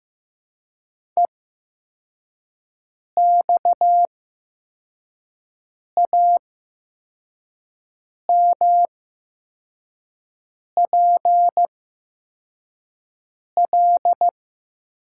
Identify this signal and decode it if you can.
Morse code, heard as a keyed tone, about 15 wpm, spaced as for 4 wpm, 700 Hz. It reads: EXAMPL
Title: 8x Character Spacing - Top 500 Words - 15wpm